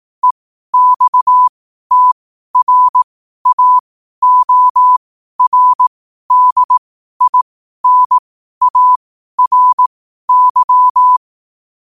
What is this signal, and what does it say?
Morse code, heard as a keyed tone, about 18 wpm, spaced as for 12 wpm, 1000 Hz. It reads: EXTRAORDINARY
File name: qotdfiles / 12wpm